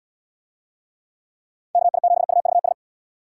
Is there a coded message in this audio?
Yes: Morse code from 1.7 to 2.7 s, heard as a keyed tone, about 71 wpm, 700 Hz. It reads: KI7UVS